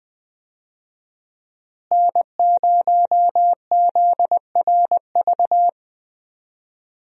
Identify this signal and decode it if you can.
Morse code, heard as a keyed tone, about 20 wpm, 700 Hz. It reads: N0ZRV
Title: USA Call Signs - 5 Characters - Repeated 3x - 20wpm